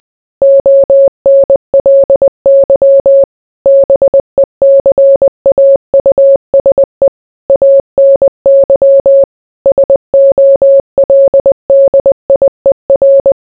Morse code, heard as a keyed tone, about 20 wpm, 550 Hz. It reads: ONLY BECAUSE ANY SOLDIER